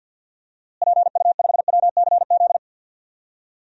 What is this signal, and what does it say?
Morse code, heard as a keyed tone, about 52 wpm, 700 Hz. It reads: PU5WCZ
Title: DXCC-100 Call Signs - 6 Characters - Repeated 3x - 50wpm